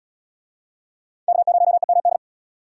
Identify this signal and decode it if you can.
Morse code, heard as a keyed tone, about 72 wpm, 700 Hz. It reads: K0EMD